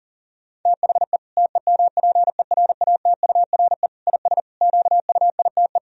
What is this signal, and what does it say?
Morse code, heard as a keyed tone, about 40 wpm, 700 Hz. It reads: THE TEMPERATURE IS QUITE
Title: Sentences from Top 1000 Words - Review Part 4 - 40wpm